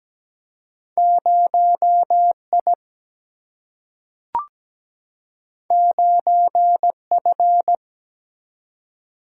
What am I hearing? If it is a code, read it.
Morse code, heard as a keyed tone, about 17 wpm, 700 Hz. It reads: 0 I 9 F
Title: Single-Letter Single-Number 17wpm